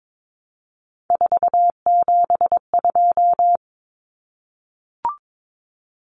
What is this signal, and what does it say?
Morse code, heard as a keyed tone, about 22 wpm, 700 Hz. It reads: 472
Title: Numbers - 3-Digit 22wpm